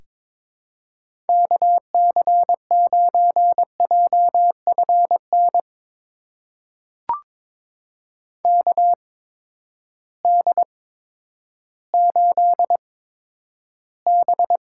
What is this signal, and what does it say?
Morse code, heard as a keyed tone, about 22 wpm, spaced as for 11 wpm, 700 Hz. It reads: KC9JFN KD8B
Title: USA Call Signs - 6 Characters - 8x Character Spacing - 22wpm